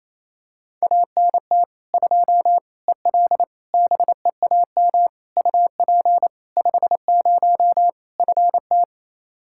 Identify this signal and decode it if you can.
Morse code, heard as a keyed tone, about 28 wpm, 700 Hz. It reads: ANT 2 EL BEAM UP 50 FT
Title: Practice QSO - Exchange 2 - Info - 28wpm